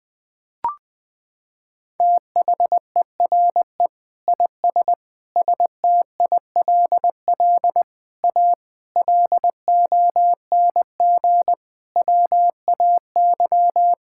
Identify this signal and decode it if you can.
Morse code, heard as a keyed tone, about 20 wpm, 700 Hz. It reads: THERE IS STILL A LONG WAY
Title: Sentences with 8-Word Phrases - 20wpm